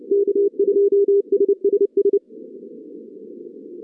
Morse code, heard as a keyed tone, about 27 wpm, 410 Hz. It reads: K2SSS
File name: 2015 cq ww cw / 2015 cqwww k5tr - 0-18